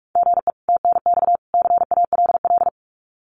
Morse code, heard as a keyed tone, about 44 wpm, 700 Hz. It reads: GE TNX CALL
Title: Common QSO Phrases 45wpm